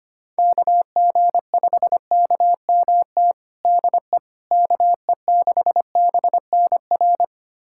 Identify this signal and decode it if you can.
Morse code, heard as a keyed tone, about 25 wpm, 700 Hz. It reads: KG5KMT DE KE6BNR
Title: USA Callsign DE Callsign - 6 Characters - 25wpm